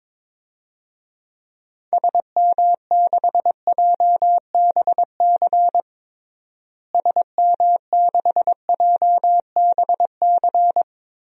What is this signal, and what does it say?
Morse code, heard as a keyed tone, about 22 wpm, 700 Hz. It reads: SM6JBC SM6JBC
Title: DXCC-100 Call Signs - 6 Characters - Repeated 2x - 22wpm